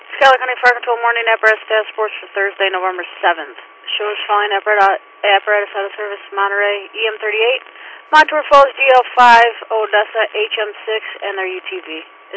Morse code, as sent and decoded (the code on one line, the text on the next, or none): none